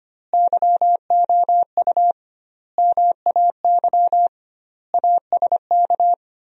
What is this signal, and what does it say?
Morse code, heard as a keyed tone, about 25 wpm, 700 Hz. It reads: YOU MAY ASK
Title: 2x Word-Spacing - Sentences from Top 200 Words - 25wpm